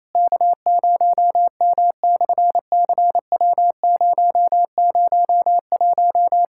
Morse code, heard as a keyed tone, about 27 wpm, 700 Hz. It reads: K0M/CW001
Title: SOTA QSO Practice 28wpm